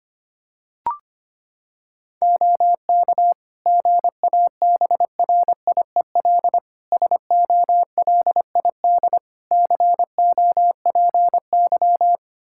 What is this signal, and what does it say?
Morse code, heard as a keyed tone, about 25 wpm, 700 Hz. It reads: OK GABRIEL SOLID COPY
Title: Practice QSO - Exchange 3 - More Info - 25wpm